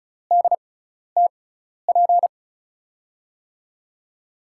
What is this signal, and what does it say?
Morse code, heard as a keyed tone, about 34 wpm, spaced as for 12 wpm, 700 Hz. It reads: DTP
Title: USA Call Signs - 6x Character Spacing - 35wpm